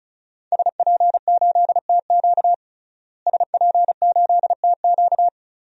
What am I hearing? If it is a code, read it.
Morse code, heard as a keyed tone, about 36 wpm, 700 Hz. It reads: SP8TQ SP8TQ
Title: DXCC-100 Call Signs - 5 Characters - Repeated 2x - 35wpm